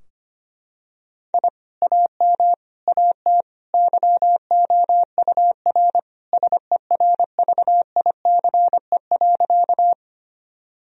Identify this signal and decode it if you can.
Morse code, heard as a keyed tone, about 25 wpm, 700 Hz. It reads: I AM AT YOUR SERVICE.
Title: Sentences from Top 1000 Words - Review Part 4 - 25wpm